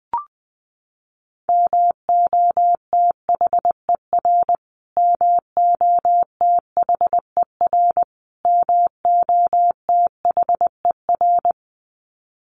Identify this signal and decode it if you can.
Morse code, heard as a keyed tone, about 20 wpm, 700 Hz. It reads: MOTHER MOTHER MOTHER
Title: Top 500 Words Repeated 20wpm